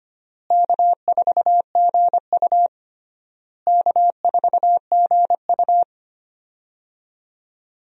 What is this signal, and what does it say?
Morse code, heard as a keyed tone, about 25 wpm, 700 Hz. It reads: K4GU K4GU
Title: USA Call Signs - 4 Characters - Repeated 3x - 25wpm